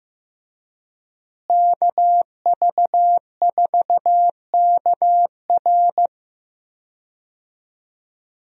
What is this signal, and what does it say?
Morse code, heard as a keyed tone, about 15 wpm, 700 Hz. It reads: KV4KR
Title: USA Call Signs - 5 Characters - Repeated 3x - 15wpm